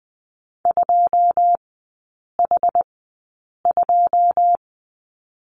Morse code, heard as a keyed tone, about 20 wpm, spaced as for 8 wpm, 690 Hz. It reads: 2H2
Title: 081 Lesson 14e - Number 2 - Three Characters 20wpm